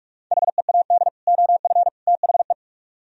Morse code, heard as a keyed tone, about 60 wpm, spaced as for 52 wpm, 700 Hz. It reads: HEAD OF THE
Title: Speed-Racing - Sets of 3 Words 45wpm